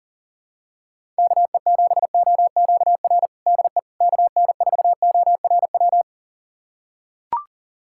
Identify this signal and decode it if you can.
Morse code, heard as a keyed tone, about 37 wpm, 700 Hz. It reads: KE7OQR DE KN4ORW
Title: USA Callsign DE Callsign - 6 Characters - 40wpm